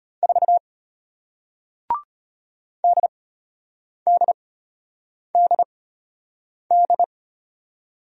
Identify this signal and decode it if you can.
Morse code, heard as a keyed tone, about 33 wpm, spaced as for 8 wpm, 700 Hz. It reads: 4 DDDD